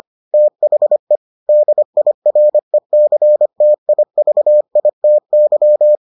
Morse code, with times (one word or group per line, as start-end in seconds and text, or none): 0.3-1.2 THE
1.5-6.0 DIRECTIVITY